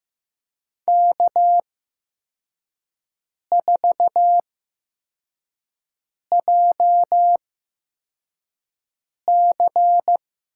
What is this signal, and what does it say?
Morse code, heard as a keyed tone, about 15 wpm, spaced as for 4 wpm, 700 Hz. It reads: K4JC
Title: USA Call Signs - 8x Character Spacing - 15wpm